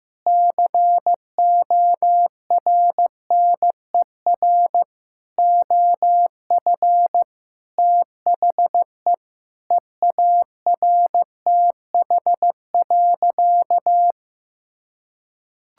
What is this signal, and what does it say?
Morse code, heard as a keyed tone, about 16 wpm, 700 Hz. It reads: CORNER OF THE EARTH.
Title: Sentences from Top 1600 Words 15wpm